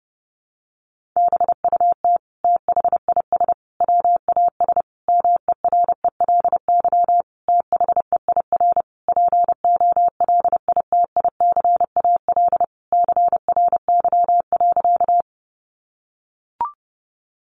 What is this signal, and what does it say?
Morse code, heard as a keyed tone, about 28 wpm, 700 Hz. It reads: BUT THIS WAS MERELY THEIR POLITICAL CRY.
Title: Sentences from Top 1400 Words 30wpm